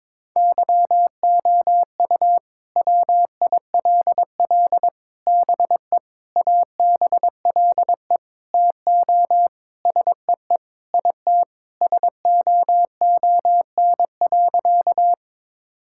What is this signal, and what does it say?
Morse code, heard as a keyed tone, about 21 wpm, 700 Hz. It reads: YOU WILL BE ABLE TO SEE IT SOON.
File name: practice / 22wpm / Sentences with 4-Word Phrases - 22wpm